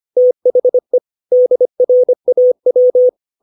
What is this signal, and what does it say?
Morse code, heard as a keyed tone, about 25 wpm, 500 Hz. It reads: THE DRAW
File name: 632 words / 632 Words 25WPM